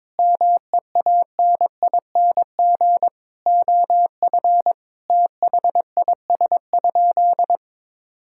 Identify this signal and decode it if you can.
Morse code, heard as a keyed tone, about 22 wpm, 700 Hz. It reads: MEANING OF THIS?
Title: Sentences from Top 1000 Words - Review Part 6 - 22wpm